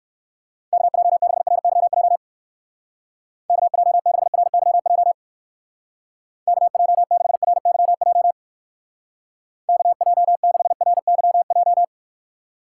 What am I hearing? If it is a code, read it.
Morse code, heard as a keyed tone, about 62 wpm, 700 Hz. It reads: KJ6RYJ KJ6RYJ KJ6RYJ KJ6RYJ